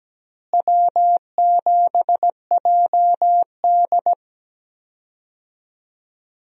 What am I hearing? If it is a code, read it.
Morse code, heard as a keyed tone, about 17 wpm, 700 Hz. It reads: W7JD
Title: USA Call Signs - 4 Characters - Repeated 3x - 17wpm